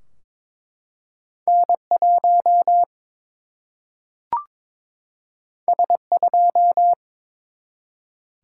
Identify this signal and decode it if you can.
Morse code, heard as a keyed tone, about 22 wpm, 700 Hz. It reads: N 1 S 2
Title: Single-Letter Single-Number 22wpm